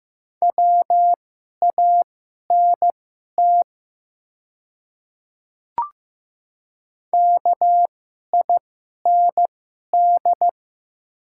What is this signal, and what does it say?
Morse code, heard as a keyed tone, about 15 wpm, spaced as for 11 wpm, 700 Hz. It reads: WANT KIND
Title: 2x Character Spacing - Top 300 Words - 15wpm